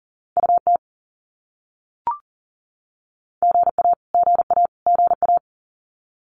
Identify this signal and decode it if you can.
Morse code, heard as a keyed tone, about 40 wpm, 700 Hz. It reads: UT GA GA GA
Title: US States Repeated 3x 40wpm